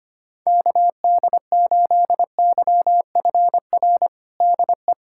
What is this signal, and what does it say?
Morse code, heard as a keyed tone, about 25 wpm, 700 Hz. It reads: KD8YFR DE
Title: USA Callsign DE Callsign 25wpm